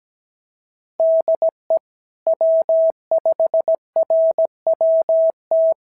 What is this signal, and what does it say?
Morse code, heard as a keyed tone, about 17 wpm, 660 Hz. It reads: DE W5RWT
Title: SOTA QSO Practice 17wpm